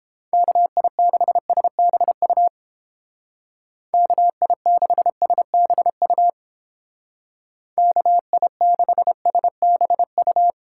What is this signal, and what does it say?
Morse code, heard as a keyed tone, about 29 wpm, 700 Hz. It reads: KI6SBU KI6SBU KI6SBU